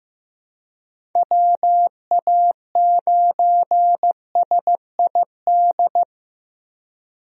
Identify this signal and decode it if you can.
Morse code, heard as a keyed tone, about 15 wpm, 700 Hz. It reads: WA9SID
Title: USA Call Signs - 6 Characters - Repeated 2x - 15wpm